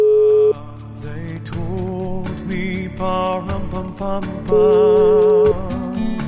Ringback tone (Europe, 425 Hz).